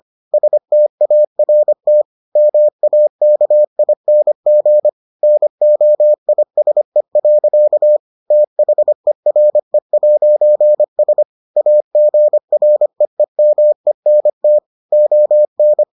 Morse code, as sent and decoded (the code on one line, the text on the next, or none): ... - .- .-. - / -- .- -.- .. -. --. / -. --- .. ... .-.-.- / - .... .-. .----. ... / .- --. .-. . -- . -. - / --- -.
START MAKING NOISE. THERE'S AGREEMENT ON